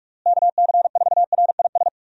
Morse code, heard as a keyed tone, about 43 wpm, 700 Hz. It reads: KC4RIS
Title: USA Call Signs - 6 Characters - Repeated 3x - 45wpm